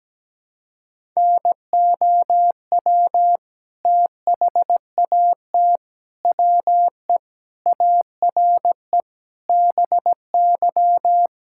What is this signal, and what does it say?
Morse code, heard as a keyed tone, about 17 wpm, 700 Hz. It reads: NOW THAT WE ARE BY
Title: Sentences from Top 1600 Words 17wpm